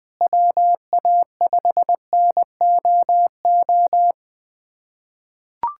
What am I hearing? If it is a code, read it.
Morse code, heard as a keyed tone, about 20 wpm, 700 Hz. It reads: WA5NOO
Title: USA Callsign DE Callsign - 6 Characters - 20wpm